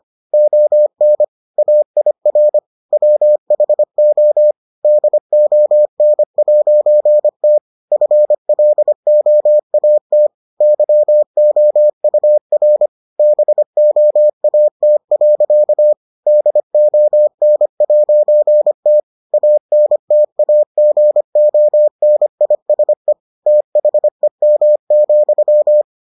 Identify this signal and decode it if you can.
Morse code, heard as a keyed tone, about 25 wpm, 600 Hz. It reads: ON AIR WHO DON'T FLOAT YOUR BOAT. DON'T ANTAGONISE THEM,